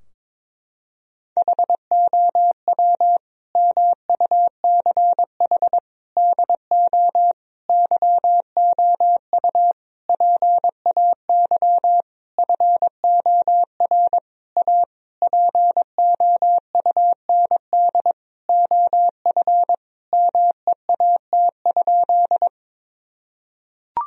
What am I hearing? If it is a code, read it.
Morse code, heard as a keyed tone, about 22 wpm, 700 Hz. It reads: HOW MUCH DO YOU PAY FOR A POUND OF MEAT?
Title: Sentences from Top 1000 Words - Review Part 3 - 22wpm